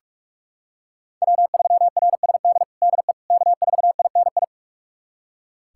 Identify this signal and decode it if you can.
Morse code, heard as a keyed tone, about 46 wpm, 700 Hz. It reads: W3RSD DE K4INI
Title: USA Callsign DE Callsign - 5 Characters - 45wpm